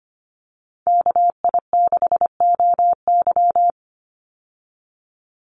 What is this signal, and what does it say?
Morse code, heard as a keyed tone, about 25 wpm, 700 Hz. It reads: KI6OY
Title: Call Signs Repeated 3x - 25wpm